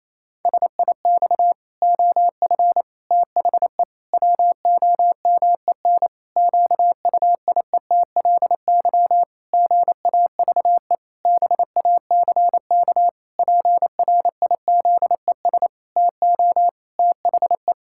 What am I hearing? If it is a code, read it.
Morse code, heard as a keyed tone, about 28 wpm, 700 Hz. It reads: SIX OF THE WOMEN QUIETLY GAVE BACK PRIZES TO THE
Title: Pangram Challenge v2 - 28wpm